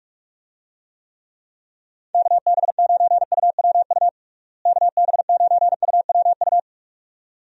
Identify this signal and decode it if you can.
Morse code, heard as a keyed tone, about 46 wpm, 700 Hz. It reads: KB9UWU KB9UWU
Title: USA Call Signs - 6 Characters - Repeated 2x - 45wpm